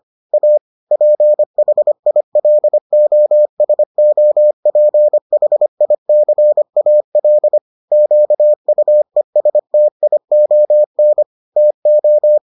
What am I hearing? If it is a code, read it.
Morse code, heard as a keyed tone, about 25 wpm, 600 Hz. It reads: A PHILOSOPHICAL QUESTION TO